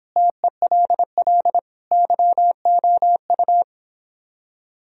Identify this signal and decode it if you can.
Morse code, heard as a keyed tone, about 29 wpm, 700 Hz. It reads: TELL YOU